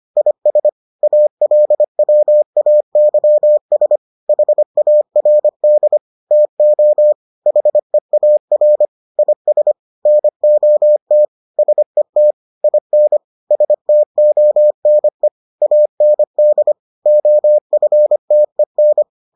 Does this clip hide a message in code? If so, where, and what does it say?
Yes: Morse code from 0.2 to 19.0 s, heard as a keyed tone, about 25 wpm, 600 Hz. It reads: IS ALWAYS HARD TO HEAR IS NOT SET IN STONE AND OFTEN